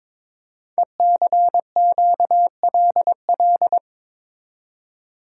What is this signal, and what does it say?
Morse code, heard as a keyed tone, about 22 wpm, 700 Hz. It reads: ECQLL
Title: Code Groups - 5 Letters - 22wpm